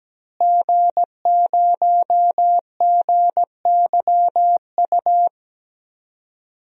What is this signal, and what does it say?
Morse code, heard as a keyed tone, about 17 wpm, 700 Hz. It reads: G0GYU
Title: DXCC-100 Call Signs - 5 Characters - Repeated 3x - 17wpm